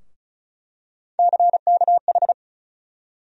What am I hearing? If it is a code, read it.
Morse code, heard as a keyed tone, about 36 wpm, 700 Hz. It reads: CKH
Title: USA Call Sign Postfixes - 3 Letters 35wpm